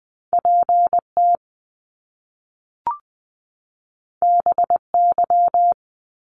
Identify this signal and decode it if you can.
Morse code, heard as a keyed tone, about 20 wpm, 700 Hz. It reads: PT BY
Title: Top Two Letter Combinations in Words 20wpm